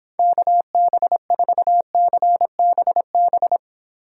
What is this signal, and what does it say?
Morse code, heard as a keyed tone, about 29 wpm, 700 Hz. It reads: KB4CBB